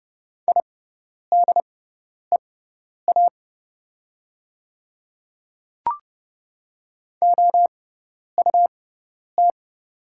Morse code, heard as a keyed tone, about 30 wpm, spaced as for 10 wpm, 700 Hz. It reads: IDEA OUT
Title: 6x Character Spacing - Top 500 Words - 30wpm